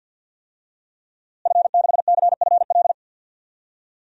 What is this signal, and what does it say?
Morse code, heard as a keyed tone, about 49 wpm, 700 Hz. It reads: F6CFL
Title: DXCC-100 Call Signs - 5 Characters - Repeated 2x - 50wpm